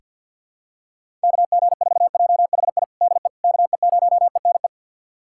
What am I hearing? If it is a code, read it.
Morse code, heard as a keyed tone, about 47 wpm, 700 Hz. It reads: KG4JHI DE KE0ENE
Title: USA Callsign DE Callsign 50wpm